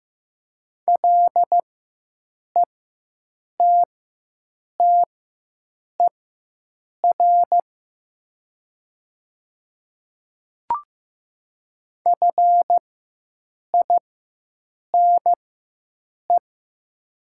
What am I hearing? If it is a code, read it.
Morse code, heard as a keyed tone, about 15 wpm, spaced as for 7 wpm, 700 Hz. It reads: LETTER FINE
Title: 4x Character Spacing - Top 500 Words - 15wpm